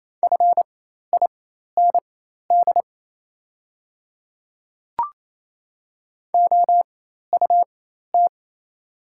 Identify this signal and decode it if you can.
Morse code, heard as a keyed tone, about 27 wpm, spaced as for 13 wpm, 690 Hz. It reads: FIND OUT
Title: 4x Character Spacing - Top 100 Words - 28wpm